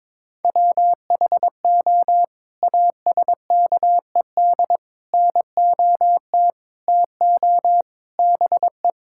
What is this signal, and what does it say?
Morse code, heard as a keyed tone, about 22 wpm, 700 Hz. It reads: WHO ASKED NOT TO BE
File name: practice / 22wpm / Sets of 7 Words - 22wpm